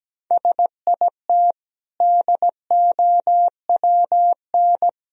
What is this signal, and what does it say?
Morse code, heard as a keyed tone, about 17 wpm, 700 Hz. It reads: SIT DOWN
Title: Sentences from Top 1000 Words - Review Part 4 - 17wpm